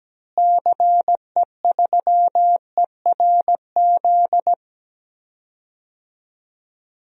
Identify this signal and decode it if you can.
Morse code, heard as a keyed tone, about 17 wpm, 700 Hz. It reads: CE3ERZ
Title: DXCC-100 Call Signs - 6 Characters - Repeated 3x - 17wpm